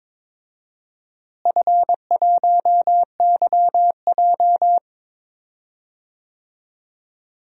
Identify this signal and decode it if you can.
Morse code, heard as a keyed tone, about 22 wpm, 700 Hz. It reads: F1YJ
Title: DXCC-100 Call Signs - 4 Characters - Repeated 2x - 22wpm